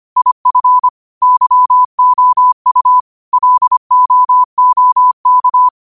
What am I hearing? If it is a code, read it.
Morse code, heard as a keyed tone, about 25 wpm, 1000 Hz. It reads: IF YOU LOOK